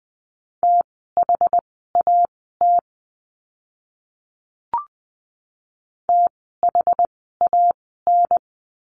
Morse code, heard as a keyed tone, about 20 wpm, spaced as for 15 wpm, 700 Hz. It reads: THAT THAN
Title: 185 Lesson 30f - Letter b - Words 20wpm